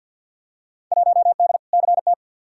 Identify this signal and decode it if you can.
Morse code, heard as a keyed tone, about 48 wpm, 700 Hz. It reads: 1D CT
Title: Field Day Exchanges - 50wpm